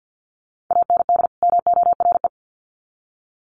Morse code, heard as a keyed tone, about 50 wpm, 700 Hz. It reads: AND MORE
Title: Sets of 2 Words 50wpm